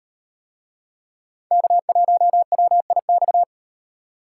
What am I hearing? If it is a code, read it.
Morse code, heard as a keyed tone, about 38 wpm, 700 Hz. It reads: K1WIX